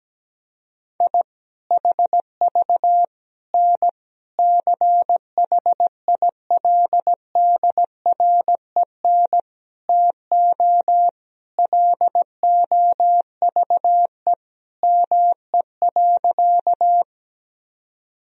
Morse code, heard as a keyed tone, about 17 wpm, 700 Hz. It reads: I HV N CHILDREN TO LOVE ME.
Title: Sentences from Top 400 Words and Abbreviations 17wpm